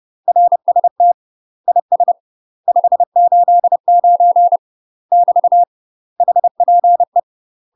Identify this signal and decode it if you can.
Morse code, heard as a keyed tone, about 30 wpm, 700 Hz. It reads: RST IS 589 <BT> HPE
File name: UK MASTER QSOs TRAINING / 30WPM / UK MASTER QSOs 0039